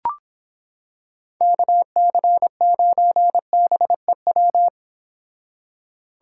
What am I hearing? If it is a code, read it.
Morse code, heard as a keyed tone, about 26 wpm, 700 Hz. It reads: KC9BEW